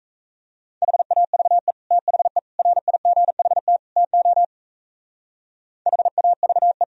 Morse code, heard as a keyed tone, about 35 wpm, 700 Hz. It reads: HAVE THE RIGHT TO HAVE
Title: Speed-Racing - Sets of 4 Words 28wpm